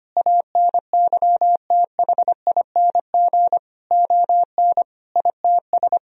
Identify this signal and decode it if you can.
Morse code, heard as a keyed tone, about 25 wpm, 700 Hz. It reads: ANYTHING ON ITS
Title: Sentences from Top 1000 Words - Review Part 3 - 25wpm